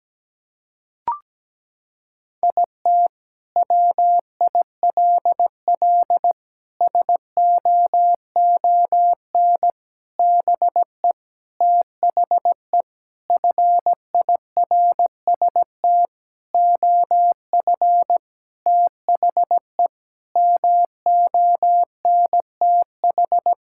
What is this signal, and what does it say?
Morse code, heard as a keyed tone, about 17 wpm, 700 Hz. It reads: IT WILL SOON BE THE FIRST OF THE MONTH
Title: Sentences from Top 1000 Words - Review Part 5 - 17wpm